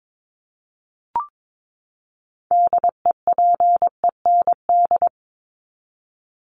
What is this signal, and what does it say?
Morse code, heard as a keyed tone, about 22 wpm, 700 Hz. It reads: DEPEND
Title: ING Suffix - Top 100 Words with Base Words 22wpm